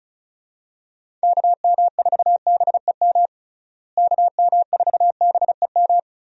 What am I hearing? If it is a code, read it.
Morse code, heard as a keyed tone, about 36 wpm, 700 Hz. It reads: KM4BEM KM4BEM
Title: USA Call Signs - 6 Characters - Repeated 2x - 35wpm